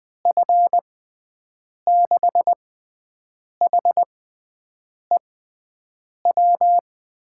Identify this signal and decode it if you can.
Morse code, heard as a keyed tone, about 20 wpm, spaced as for 7 wpm, 700 Hz. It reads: F6HEW
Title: USA Call Signs - 6 Characters - 6x Character Spacing - 20wpm